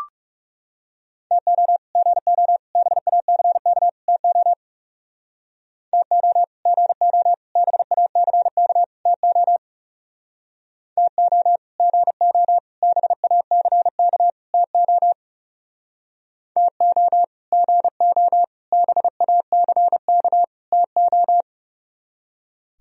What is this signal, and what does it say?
Morse code, heard as a keyed tone, about 38 wpm, 700 Hz. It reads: TO GO BACK TO TO GO BACK TO TO GO BACK TO TO GO BACK TO